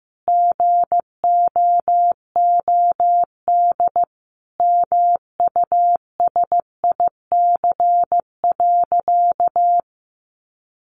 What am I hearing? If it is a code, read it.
Morse code, heard as a keyed tone, about 15 wpm, 700 Hz. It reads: GOOD MUSIC.
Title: Easy Phrases 15wpm